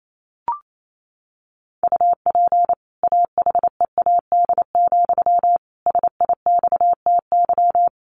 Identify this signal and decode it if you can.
Morse code, heard as a keyed tone, about 28 wpm, 700 Hz. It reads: UP AHEAD, SIXTY
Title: A Sound Of Thunder 28wpm